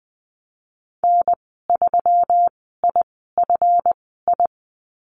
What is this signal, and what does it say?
Morse code, heard as a keyed tone, about 20 wpm, spaced as for 15 wpm, 700 Hz. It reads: N3IFI
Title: 179 Lesson 29g - Slash - Call Signs 20wpm